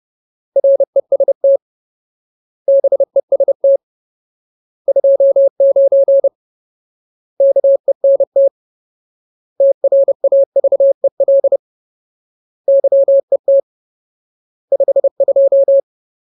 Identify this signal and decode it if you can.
Morse code, heard as a keyed tone, about 30 wpm, 550 Hz. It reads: REST BEST 29 KENT TRAVEL YET 52
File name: Ultimate Word List / 30WPM Ultimate